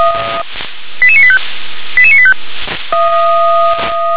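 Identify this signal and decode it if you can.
Five-tone selective call (ZVEI), heard as quick runs of five tones, tones of about 70 ms. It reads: 88985 88985
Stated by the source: ZVEI five-tone selective call 88922 received by SDR.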